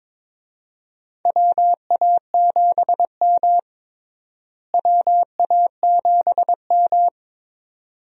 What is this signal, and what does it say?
Morse code, heard as a keyed tone, about 22 wpm, 700 Hz. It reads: WA7M WA7M
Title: USA Call Signs - 4 Characters - Repeated 2x - 22wpm